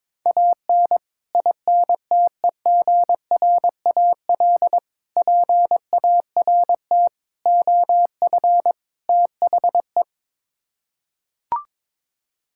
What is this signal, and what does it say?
Morse code, heard as a keyed tone, about 22 wpm, 700 Hz. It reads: AN INTEGRAL PART OF THE
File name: practice / 22wpm / Build A Phrase - 5-Word Phrases - 22wpm